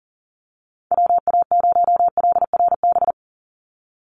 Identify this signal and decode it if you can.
Morse code, heard as a keyed tone, about 40 wpm, 700 Hz. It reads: WA0LRB
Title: USA Call Signs 40wpm